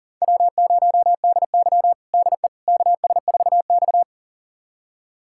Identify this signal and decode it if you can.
Morse code, heard as a keyed tone, about 40 wpm, 700 Hz. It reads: W0DY DE KS4X
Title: USA Callsign DE Callsign - 4 Characters - 40wpm